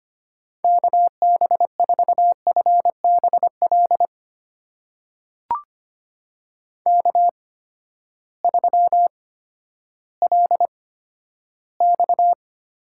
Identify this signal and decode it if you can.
Morse code, heard as a keyed tone, about 25 wpm, spaced as for 13 wpm, 700 Hz. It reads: KB4FBL K3LX
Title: USA Call Signs - 8x Character Spacing - 25wpm